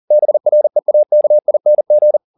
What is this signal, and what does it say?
Morse code, heard as a keyed tone, about 40 wpm, 600 Hz. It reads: BREAKING